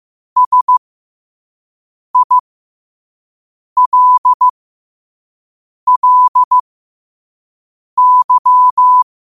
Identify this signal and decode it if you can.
Morse code, heard as a keyed tone, about 15 wpm, spaced as for 5 wpm, 1000 Hz. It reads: SILLY
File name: qotdfiles / 5wpm